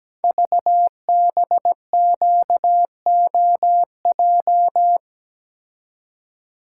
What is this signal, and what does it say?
Morse code, heard as a keyed tone, about 17 wpm, 700 Hz. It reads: VBQOJ
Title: Code Groups - 5 Letters - 17wpm